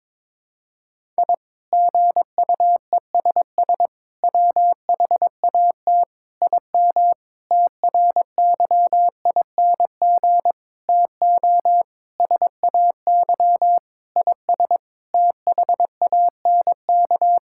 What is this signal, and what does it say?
Morse code, heard as a keyed tone, about 22 wpm, 700 Hz. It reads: I GUESS WHAT IM TRYING TO SAY IS THANK
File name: practice / 22wpm / Sentences with 8-Word Phrases - 22wpm